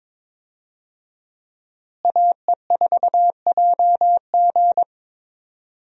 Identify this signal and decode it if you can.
Morse code, heard as a keyed tone, about 22 wpm, 700 Hz. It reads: AE4JG
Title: USA Call Signs - 5 Characters - Repeated 3x - 22wpm